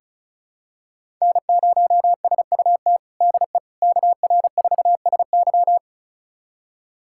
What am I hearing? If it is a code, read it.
Morse code, heard as a keyed tone, about 36 wpm, 700 Hz. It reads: N0SUT DE KR4SY
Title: USA Callsign DE Callsign - 5 Characters - 35wpm